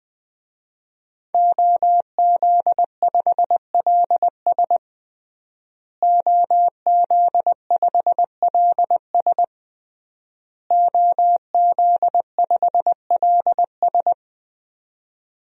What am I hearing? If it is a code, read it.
Morse code, heard as a keyed tone, about 20 wpm, 700 Hz. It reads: OZ5LS OZ5LS OZ5LS
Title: DXCC-100 Call Signs - 5 Characters - Repeated 3x - 20wpm